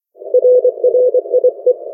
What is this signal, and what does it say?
Morse code, heard as a keyed tone, about 24 wpm, 500 Hz. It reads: RRIE
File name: RRIE-Net-2mins